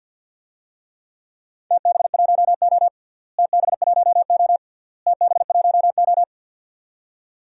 Morse code, heard as a keyed tone, about 51 wpm, 700 Hz. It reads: TB1O TB1O TB1O
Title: DXCC-100 Call Signs - 4 Characters - Repeated 3x - 50wpm